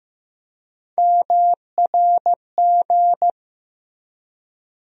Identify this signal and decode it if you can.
Morse code, heard as a keyed tone, about 15 wpm, 700 Hz. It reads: MRG
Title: USA Call Sign Postfixes - 3 Letters 15wpm